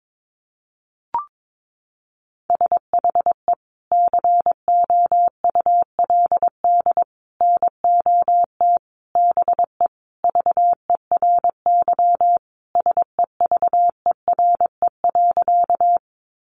Morse code, heard as a keyed tone, about 22 wpm, 700 Hz. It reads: SHE COULD NOT BE VERY SEVERE.
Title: Sentences from Top 1400 Words 22wpm